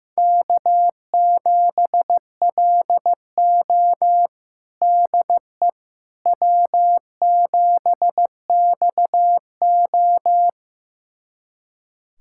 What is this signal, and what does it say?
Morse code, heard as a keyed tone, about 16 wpm, 700 Hz. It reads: K7LO DE W7XO
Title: USA Callsign DE Callsign - 4 Characters - 15wpm